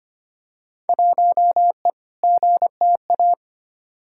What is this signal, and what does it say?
Morse code, heard as a keyed tone, about 25 wpm, 700 Hz. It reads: 1E GTA
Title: Field Day Exchanges - 25wpm